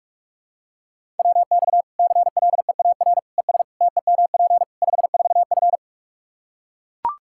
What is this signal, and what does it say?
Morse code, heard as a keyed tone, about 41 wpm, 700 Hz. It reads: WX CLEAR ES TEMP 54F
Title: Practice QSO - Exchange 2 - Info - 45wpm